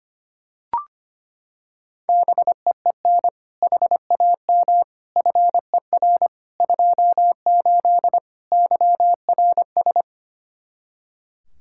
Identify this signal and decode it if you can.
Morse code, heard as a keyed tone, about 25 wpm, 700 Hz. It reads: BEEN HAM FER 28 YRS
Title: K1USN SST QSO Practice 25wpm